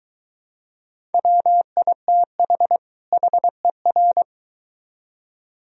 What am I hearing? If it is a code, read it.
Morse code, heard as a keyed tone, about 23 wpm, 700 Hz. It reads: WITH HER